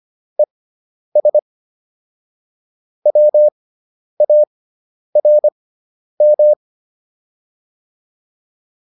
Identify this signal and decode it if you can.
Morse code, heard as a keyed tone, about 25 wpm, spaced as for 10 wpm, 600 Hz. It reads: ES WARM
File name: QSO309 10